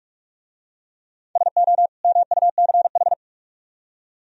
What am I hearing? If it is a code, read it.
Morse code, heard as a keyed tone, about 44 wpm, 700 Hz. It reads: SO MUCH